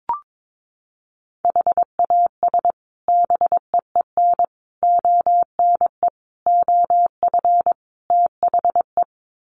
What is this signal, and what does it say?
Morse code, heard as a keyed tone, about 22 wpm, 700 Hz. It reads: HAS BEEN ONE OF THE
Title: Sets of 5 Words Encore 22wpm